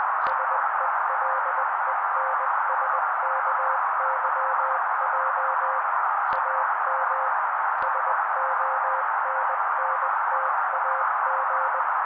Morse code, heard as a keyed tone, about 20 wpm, 530 Hz. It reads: SELENSKYJ AM SONNTAG
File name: daily / 18wpm noise